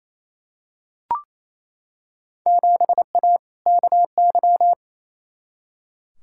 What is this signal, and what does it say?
Morse code, heard as a keyed tone, about 28 wpm, 700 Hz. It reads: 7A KY
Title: Field Day Exchanges Encore - 28wpm